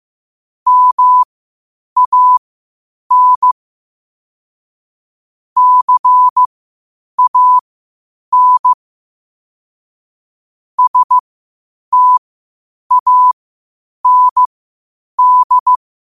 Morse code, heard as a keyed tone, about 15 wpm, spaced as for 8 wpm, 1000 Hz. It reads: MAN CAN STAND
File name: qotdfiles / 8wpm